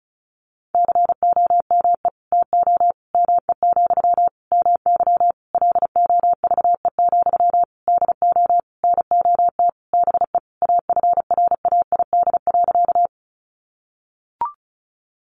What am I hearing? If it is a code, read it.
Morse code, heard as a keyed tone, about 34 wpm, 700 Hz. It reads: COME TO ME, MY LOVE, DO NOT BE AFRAID.
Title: Sentences from Top 1500 Words 35wpm